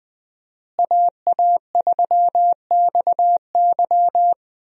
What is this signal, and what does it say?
Morse code, heard as a keyed tone, about 20 wpm, 700 Hz. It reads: AA3XY
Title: USA Call Signs - 5 Characters - Repeated 2x - 20wpm